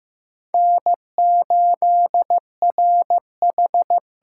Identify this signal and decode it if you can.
Morse code, heard as a keyed tone, about 15 wpm, 700 Hz. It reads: N8RH
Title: USA Call Signs - 5 Characters - 6x Character Spacing - 15wpm